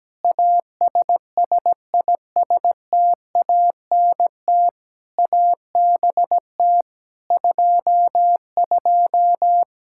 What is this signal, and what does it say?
Morse code, heard as a keyed tone, about 17 wpm, 700 Hz. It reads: ASSISTANT ABT 22
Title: Practice QSO - Exchange 3 - More Info - 17wpm